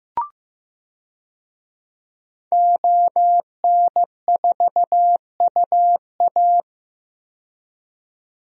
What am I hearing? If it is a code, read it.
Morse code, heard as a keyed tone, about 15 wpm, 700 Hz. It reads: ON4UA
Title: DXCC-100 Call Signs - 5 Characters - 15wpm